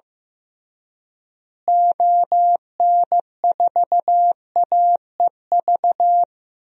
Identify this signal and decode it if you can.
Morse code, heard as a keyed tone, about 15 wpm, 700 Hz. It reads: ON4AEV
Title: DXCC-100 Call Signs - 6 Characters - Repeated 3x - 15wpm